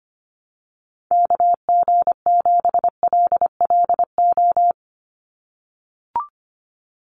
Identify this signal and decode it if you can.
Morse code, heard as a keyed tone, about 25 wpm, 690 Hz. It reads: KG7LLO
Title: USA Call Signs 25wpm